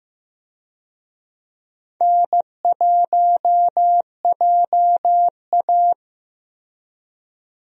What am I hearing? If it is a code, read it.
Morse code, heard as a keyed tone, about 15 wpm, 700 Hz. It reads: N1JA
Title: USA Call Signs - 4 Characters - 15wpm